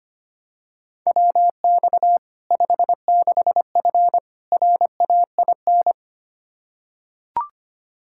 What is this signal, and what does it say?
Morse code, heard as a keyed tone, about 25 wpm, 700 Hz. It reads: WX 56F RAIN
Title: Practice QSO - Exchange 2 - Info - 25wpm